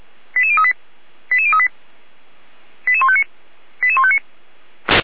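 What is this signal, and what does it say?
Five-tone selective call (ZVEI), heard as quick runs of five tones, tones of about 70 ms. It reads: 88038 88038 88159 88159
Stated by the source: Five-tone ZVEI selective call received on an SDR.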